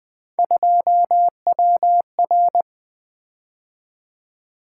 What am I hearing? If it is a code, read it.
Morse code, heard as a keyed tone, about 20 wpm, 700 Hz. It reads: 2WR
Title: Three Letter-Number Combinations - 20wpm